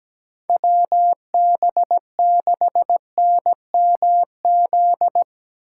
Morse code, heard as a keyed tone, about 17 wpm, 700 Hz. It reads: WB6NMZ
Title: USA Call Signs - 6 Characters - 2x Character Spacing - 17wpm